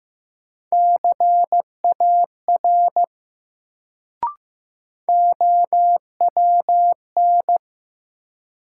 Morse code, heard as a keyed tone, about 15 wpm, 700 Hz. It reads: CAR OWN
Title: Sentences from Top 300 Words - New Word Drill - 15wpm